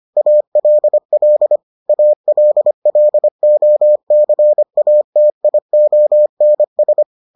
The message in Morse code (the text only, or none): ALL ALLOCATIONS